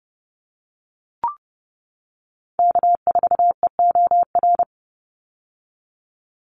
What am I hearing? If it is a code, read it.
Morse code, heard as a keyed tone, about 30 wpm, 700 Hz. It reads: K4EOR
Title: Call Signs - 30wpm